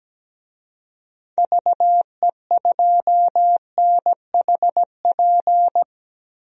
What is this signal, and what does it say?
Morse code, heard as a keyed tone, about 17 wpm, 700 Hz. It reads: VE2NHP
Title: DXCC-100 Call Signs - 6 Characters - 17wpm